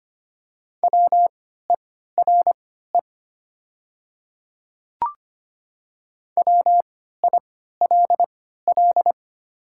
Morse code, heard as a keyed tone, about 25 wpm, spaced as for 14 wpm, 700 Hz. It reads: WERE WILL
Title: 3x Character Spacing - Top 100 Words - 25wpm